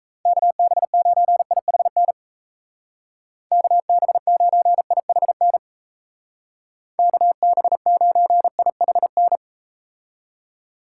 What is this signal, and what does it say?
Morse code, heard as a keyed tone, about 38 wpm, 700 Hz. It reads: KB9IHN KB9IHN KB9IHN